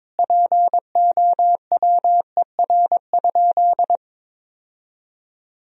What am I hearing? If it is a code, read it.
Morse code, heard as a keyed tone, about 22 wpm, 700 Hz. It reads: POWER?
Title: Sentences from Top 1000 Words - Review Part 2 - 22wpm